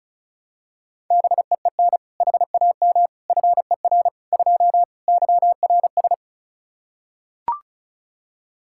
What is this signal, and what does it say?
Morse code, heard as a keyed tone, about 36 wpm, 700 Hz. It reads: BEEN HAM FER 2 YRS
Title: Practice QSO - Exchange 3 - More Info - 35wpm